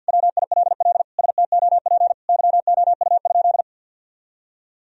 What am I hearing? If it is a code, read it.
Morse code, heard as a keyed tone, about 51 wpm, 700 Hz. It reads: WILL STOP YOU?
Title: Sentences from Top 1000 Words - Review Part 1 - 50wpm